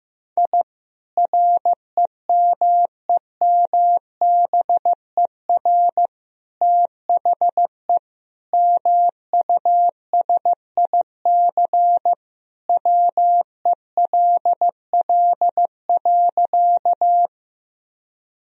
Morse code, heard as a keyed tone, about 15 wpm, 700 Hz. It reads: I REMEMBER THE MUSIC WELL.
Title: Sentences from Top 1000 Words - Review Part 1 - 15wpm